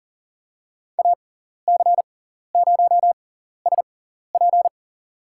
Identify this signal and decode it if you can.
Morse code, heard as a keyed tone, about 40 wpm, spaced as for 14 wpm, 700 Hz. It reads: AC0SP
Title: USA Call Signs - 5 Characters - 6x Character Spacing - 40wpm